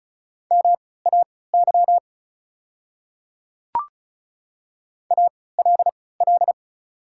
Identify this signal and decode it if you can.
Morse code, heard as a keyed tone, about 34 wpm, spaced as for 20 wpm, 700 Hz. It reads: MAY ALL
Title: 3x Character Spacing - Top 100 Words - 35wpm